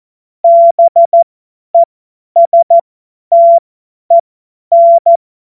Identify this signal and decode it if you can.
Morse code, heard as a keyed tone, about 14 wpm, spaced as for 10 wpm, 680 Hz. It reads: BESTEN
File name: Morsekurs / Witze / Witz 01. Februar 2022 50 F70